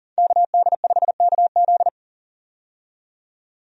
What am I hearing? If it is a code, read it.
Morse code, heard as a keyed tone, about 40 wpm, 700 Hz. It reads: KD5KZ
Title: USA Call Signs - 5 Characters - 40wpm